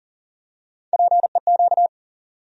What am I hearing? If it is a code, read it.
Morse code, heard as a keyed tone, about 38 wpm, 700 Hz. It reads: PEQ
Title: USA Call Sign Postfixes - 3 Letters 40wpm